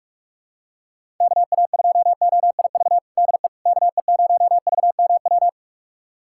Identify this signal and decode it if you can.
Morse code, heard as a keyed tone, about 43 wpm, 700 Hz. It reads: KA2OIV DE KE0VMW
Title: USA Callsign DE Callsign - 6 Characters - 45wpm